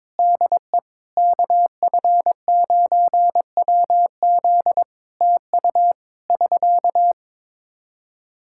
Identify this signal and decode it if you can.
Morse code, heard as a keyed tone, about 22 wpm, 700 Hz. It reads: DE KF9WZ TU <SK>
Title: K1USN SST QSO Practice 22wpm